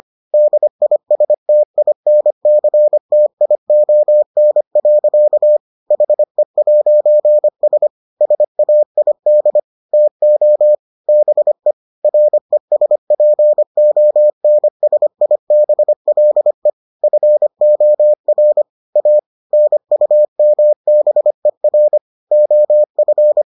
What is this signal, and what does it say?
Morse code, heard as a keyed tone, about 25 wpm, 600 Hz. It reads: DISTINCTION. HE'S SAID TO BE RESPONSIBLE FOR A NUMBER OF